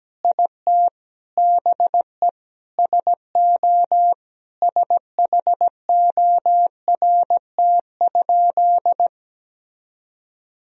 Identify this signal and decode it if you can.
Morse code, heard as a keyed tone, about 17 wpm, 700 Hz. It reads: IT BE SO SHORT?
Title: Sentences from Top 1000 Words - Review Part 2 - 17wpm